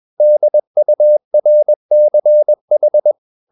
Morse code, heard as a keyed tone, about 21 wpm, 600 Hz. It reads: DURCH